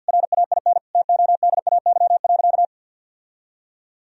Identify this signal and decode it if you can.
Morse code, heard as a keyed tone, about 48 wpm, 700 Hz. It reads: RAIN TODAY.
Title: Sentences from Top 1000 Words - Review Part 1 - 50wpm